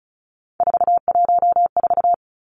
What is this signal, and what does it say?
Morse code, heard as a keyed tone, about 34 wpm, 700 Hz. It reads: 414
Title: Numbers - 3-Digit 35wpm